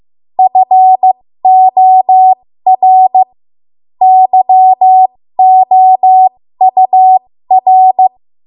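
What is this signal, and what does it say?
Morse code, heard as a keyed tone, about 15 wpm, spaced as for 13 wpm, 750 Hz. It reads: FOR YOUR
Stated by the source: 13 wpm Morse code practice archive transmission.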